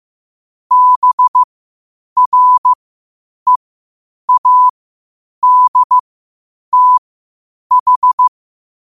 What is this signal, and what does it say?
Morse code, heard as a keyed tone, about 15 wpm, spaced as for 8 wpm, 1000 Hz. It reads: BREADTH
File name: qotdfiles / 8wpm